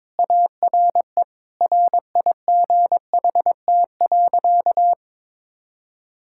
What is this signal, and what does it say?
Morse code, heard as a keyed tone, about 23 wpm, 700 Hz. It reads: ARE RIGHT.
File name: practice / 22wpm / Sentences with 3-Word Phrases - 22wpm